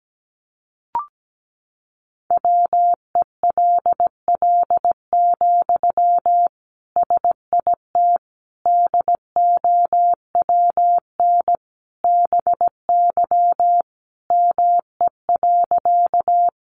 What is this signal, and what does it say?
Morse code, heard as a keyed tone, about 17 wpm, 700 Hz. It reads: WELL, SIT DOWN BY ME.
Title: Sentences from Top 500 Words 17wpm